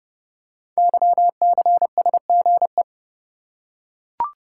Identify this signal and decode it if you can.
Morse code, heard as a keyed tone, about 30 wpm, 700 Hz. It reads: YCSGE
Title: Code Groups - 5 Letters - 30wpm